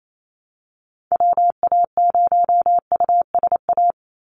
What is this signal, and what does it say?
Morse code, heard as a keyed tone, about 28 wpm, 700 Hz. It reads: WA0USA
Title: Call Signs Repeated 3x - 28wpm